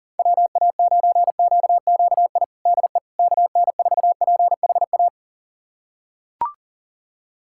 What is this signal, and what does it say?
Morse code, heard as a keyed tone, about 40 wpm, 700 Hz. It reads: WA9QQI DE KN4PHA
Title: USA Callsign DE Callsign 40wpm